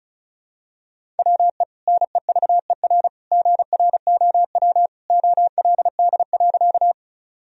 Morse code, heard as a keyed tone, about 36 wpm, 700 Hz. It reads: WE NEVER GROW OLD.
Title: Sentences from Top 1000 Words - Review Part 1 - 35wpm